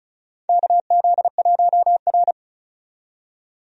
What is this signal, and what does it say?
Morse code, heard as a keyed tone, about 36 wpm, 700 Hz. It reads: KZ1R
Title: USA Call Signs - 4 Characters - 35wpm